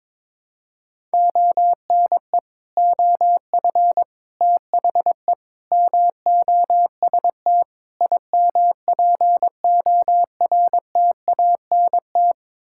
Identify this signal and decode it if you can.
Morse code, heard as a keyed tone, about 21 wpm, 700 Hz. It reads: ONE OF THE MOST IMPORTANT
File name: practice / 22wpm / Build A Phrase - 5-Word Phrases - 22wpm